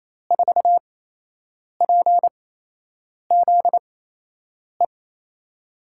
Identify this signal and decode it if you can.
Morse code, heard as a keyed tone, about 28 wpm, 700 Hz. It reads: 4 P Z E
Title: USA Call Signs - 8x Character Spacing - 28wpm